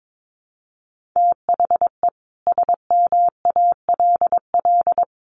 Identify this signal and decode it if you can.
Morse code, heard as a keyed tone, about 22 wpm, 700 Hz. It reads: THE SMALL
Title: Sentences from Top 1400 Words 22wpm